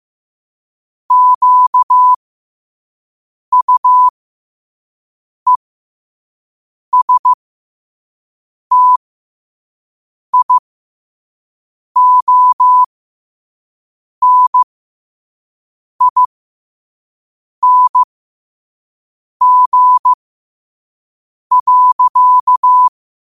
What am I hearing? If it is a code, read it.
Morse code, heard as a keyed tone, about 15 wpm, spaced as for 5 wpm, 1000 Hz. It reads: QUESTIONING.